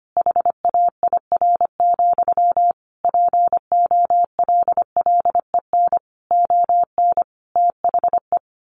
Morse code, heard as a keyed tone, about 25 wpm, 700 Hz. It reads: HAIR, POLLEN ON THE
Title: A Sound Of Thunder 25wpm